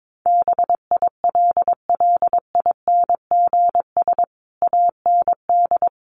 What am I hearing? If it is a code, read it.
Morse code, heard as a keyed tone, about 22 wpm, 700 Hz. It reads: BILLINGS AND
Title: A Sound Of Thunder 22wpm